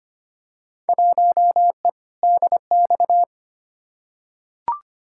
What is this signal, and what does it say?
Morse code, heard as a keyed tone, about 25 wpm, 700 Hz. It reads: 1E DX
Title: Field Day Exchanges Encore - 25wpm